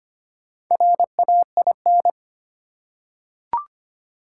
Morse code, heard as a keyed tone, about 25 wpm, 700 Hz. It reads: RAIN
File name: practice / 25wpm / Easy Words with Hints - 25wpm